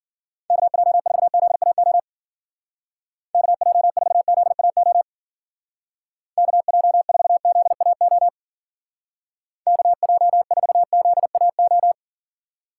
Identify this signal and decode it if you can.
Morse code, heard as a keyed tone, about 49 wpm, 700 Hz. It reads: KJ4ZAO KJ4ZAO KJ4ZAO KJ4ZAO
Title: Speed Racing - USA 2x3 Call Signs 40wpm